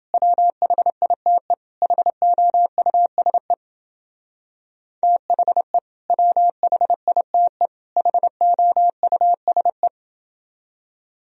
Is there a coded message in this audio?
Yes: Morse code from 0.1 to 9.9 s, heard as a keyed tone, about 28 wpm, 700 Hz. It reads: WHITE HOUSE THE WHITE HOUSE